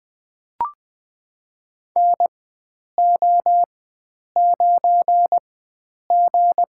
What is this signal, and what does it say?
Morse code, heard as a keyed tone, about 20 wpm, spaced as for 9 wpm, 700 Hz. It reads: NO9G
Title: USA Call Signs - 4 Characters - 4x Character Spacing - 20wpm